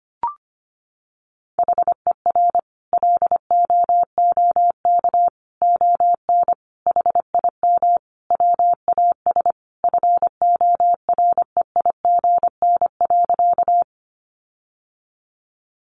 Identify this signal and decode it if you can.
Morse code, heard as a keyed tone, about 25 wpm, 700 Hz. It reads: HER LOOK ON HIM WAS FOREIGN.
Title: Sentences from Top 1400 Words 25wpm